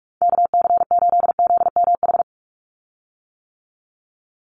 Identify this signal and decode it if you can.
Morse code, heard as a keyed tone, about 44 wpm, 700 Hz. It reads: KC8ZMH
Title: USA Call Signs 45wpm